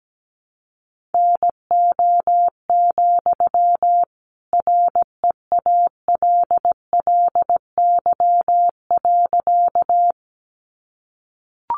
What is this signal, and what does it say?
Morse code, heard as a keyed tone, about 17 wpm, 700 Hz. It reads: NO, REALLY.
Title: Sentences from Top 600 Words 17wpm